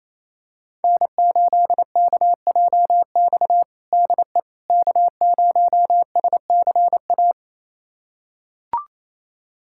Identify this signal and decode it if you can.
Morse code, heard as a keyed tone, about 27 wpm, 700 Hz. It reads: N8KJX DE K0SCA
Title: USA Callsign DE Callsign - 5 Characters - 28wpm